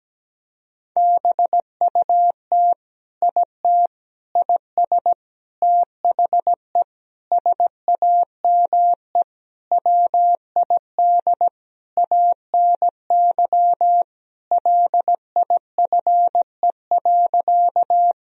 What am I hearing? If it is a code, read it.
Morse code, heard as a keyed tone, about 17 wpm, 700 Hz. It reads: BUT IT IS THE SAME WID ANY LIFE.
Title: Sentences from Top 300 Words and Abbreviations 17wpm